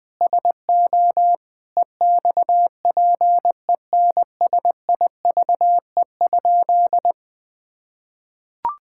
Morse code, heard as a keyed tone, about 20 wpm, 700 Hz. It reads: SO EXPENSIVE?
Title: Sentences from Top 1000 Words - Review Part 6 - 20wpm